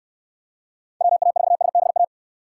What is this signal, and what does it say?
Morse code, heard as a keyed tone, about 66 wpm, 700 Hz. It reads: WA4SBA